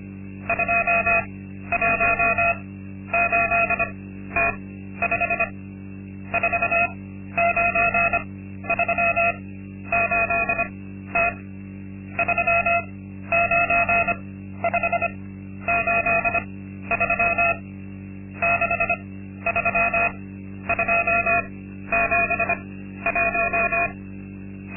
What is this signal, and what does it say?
Morse code, heard as a keyed tone, about 25 wpm, spaced as for 12 wpm, 680 Hz. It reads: 218T5 4938T 39583 63271
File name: Linksmakalnis-2